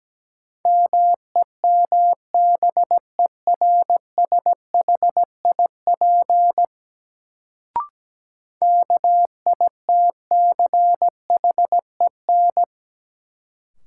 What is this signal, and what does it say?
Morse code, heard as a keyed tone, about 17 wpm, 700 Hz. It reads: MEMBERSHIP KITCHEN
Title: Sentences from Top 1400 Words - New Word Drill - 17wpm